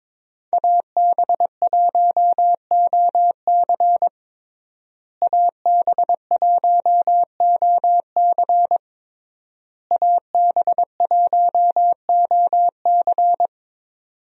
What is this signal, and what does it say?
Morse code, heard as a keyed tone, about 21 wpm, 700 Hz. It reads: AB1OC AB1OC AB1OC
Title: USA Call Signs - 5 Characters - Repeated 3x - 22wpm